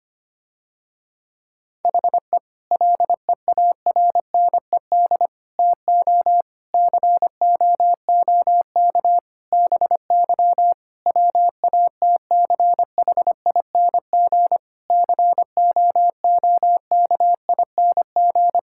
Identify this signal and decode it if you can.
Morse code, heard as a keyed tone, about 25 wpm, 700 Hz. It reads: HE LEARNED TO COOK BY WATCHING COOKING
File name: practice / 25wpm / Difficult Sentences with Hints - 25wpm